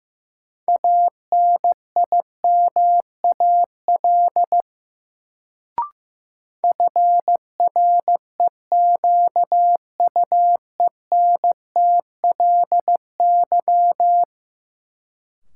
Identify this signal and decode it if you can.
Morse code, heard as a keyed tone, about 15 wpm, 700 Hz. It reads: ANIMAL FREQUENTLY
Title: Sentences from Top 1300 Words - New Word Drill Alt - 15wpm